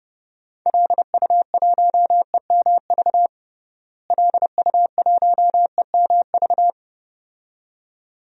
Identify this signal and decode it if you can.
Morse code, heard as a keyed tone, about 30 wpm, 700 Hz. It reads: LU1EMV LU1EMV
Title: DXCC-100 Call Signs - 6 Characters - Repeated 2x - 30wpm